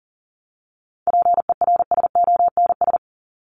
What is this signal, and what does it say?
Morse code, heard as a keyed tone, about 40 wpm, 700 Hz. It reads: PERSONS
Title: ES:S Suffix - Top 100 Words with Base Words 40wpm